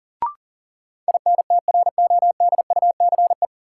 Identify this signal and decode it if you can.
Morse code, heard as a keyed tone, about 40 wpm, 700 Hz. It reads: INTRODUCE
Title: Sentences from Top 1600 Words - New Word Drill Alt - 40wpm